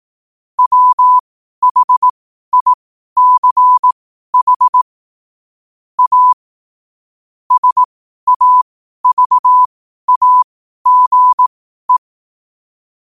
Morse code, heard as a keyed tone, about 18 wpm, spaced as for 12 wpm, 1000 Hz. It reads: WHICH A SAVAGE
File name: qotdfiles / 12wpm